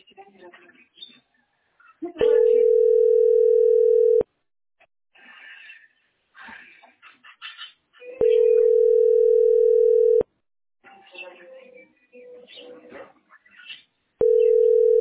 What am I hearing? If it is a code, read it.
Ringback tone (North America), 440 + 480 Hz, with speech.
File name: 3f14cd75-226b-4dda-849e